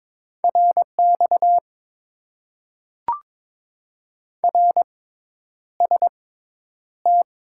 Morse code, heard as a keyed tone, about 21 wpm, spaced as for 10 wpm, 700 Hz. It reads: RX RST
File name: Top 90 QSO Elements - 6x Character Spacing - 22wpm